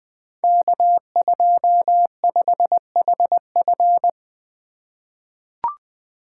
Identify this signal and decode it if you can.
Morse code, heard as a keyed tone, about 20 wpm, 700 Hz. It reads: K25HF
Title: Code Groups - 5 Letter:Numbers - 20wpm